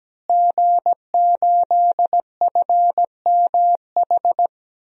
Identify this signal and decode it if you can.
Morse code, heard as a keyed tone, about 17 wpm, 700 Hz. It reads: G8FMH
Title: DXCC-100 Call Signs - 5 Characters - Repeated 3x - 17wpm